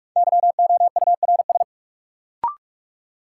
Morse code, heard as a keyed tone, about 46 wpm, 700 Hz. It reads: YOURS
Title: Sentences from Top 1500 Words - New Word Drill - 45wpm